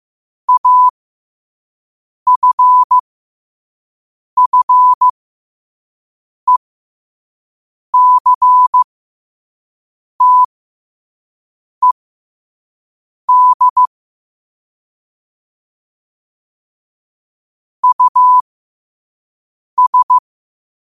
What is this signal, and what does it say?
Morse code, heard as a keyed tone, about 15 wpm, spaced as for 5 wpm, 1000 Hz. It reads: AFFECTED US
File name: qotdfiles / 5wpm